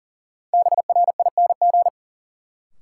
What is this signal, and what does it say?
Morse code, heard as a keyed tone, about 37 wpm, 700 Hz. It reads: BRING
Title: Sets of 5 Words Encore - New Word Drill - 40wpm